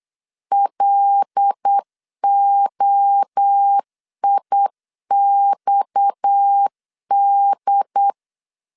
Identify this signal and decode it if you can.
Morse code, heard as a keyed tone, about 8 wpm, 800 Hz. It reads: LOIXD
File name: LEZIONI / 6 PKX0 / 6 PKX 8wpm 3041